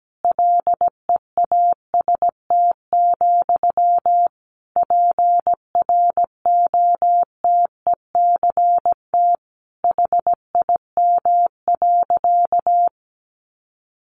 Morse code, heard as a keyed tone, about 17 wpm, 700 Hz. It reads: LEAST, PROTECT HIM.
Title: Sentences from Top 900 Words 17wpm